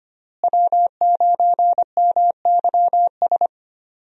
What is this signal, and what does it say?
Morse code, heard as a keyed tone, about 25 wpm, 690 Hz. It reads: W9MYS
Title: USA Call Signs - 5 Characters - Repeated 2x - 25wpm